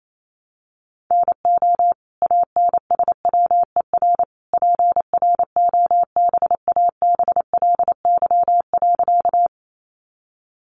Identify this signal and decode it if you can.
Morse code, heard as a keyed tone, about 28 wpm, 700 Hz. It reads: NO ANSWER PROBABLY.
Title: Sentences from Top 700 Words 28wpm